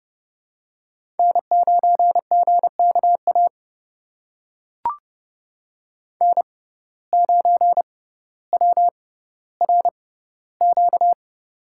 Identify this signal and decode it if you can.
Morse code, heard as a keyed tone, about 30 wpm, spaced as for 15 wpm, 700 Hz. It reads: N9GKA N9WRQ
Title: USA Call Signs - 5 Characters - 6x Character Spacing - 30wpm